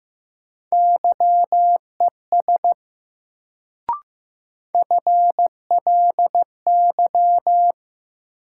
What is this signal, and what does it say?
Morse code, heard as a keyed tone, about 15 wpm, 700 Hz. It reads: YES FLY
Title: Sentences from Top 500 Words - New Word Drill - 15wpm